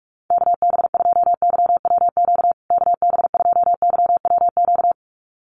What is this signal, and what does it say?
Morse code, heard as a keyed tone, about 44 wpm, 700 Hz. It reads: KB2YWQ KB2YWQ
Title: USA Call Signs Repeated 2x 45wpm